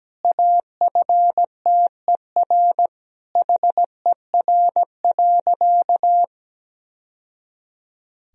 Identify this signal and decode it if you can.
Morse code, heard as a keyed tone, about 17 wpm, 700 Hz. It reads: AFTER HER.
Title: Sentences from Top 1000 Words - Review Part 1 - 17wpm